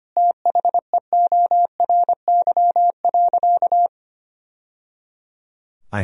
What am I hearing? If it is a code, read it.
Morse code, heard as a keyed tone, about 25 wpm, 700 Hz. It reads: THEORY.
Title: Sentences from Top 1000 Words - Review Part 3 - 25wpm